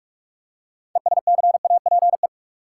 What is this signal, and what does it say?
Morse code, heard as a keyed tone, about 45 wpm, 700 Hz. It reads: ESCAPE
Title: Six Letter Words - 45wpm